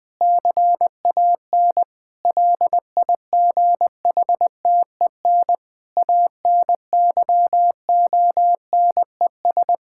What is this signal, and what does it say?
Morse code, heard as a keyed tone, about 20 wpm, 700 Hz. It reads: CAN LIGHTEN ANYONES
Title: Difficult Sentences with Hints - 20wpm